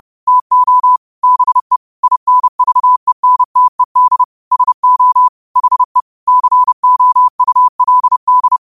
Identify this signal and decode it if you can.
Morse code, heard as a keyed tone, about 30 wpm, 1000 Hz. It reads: TO BE INVENTED SO HE COULD
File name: qotdfiles / 30wpm